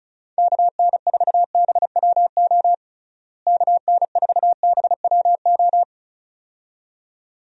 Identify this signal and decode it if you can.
Morse code, heard as a keyed tone, about 34 wpm, 700 Hz. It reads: KN4BWO KN4BWO
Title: USA Call Signs - 6 Characters - Repeated 2x - 35wpm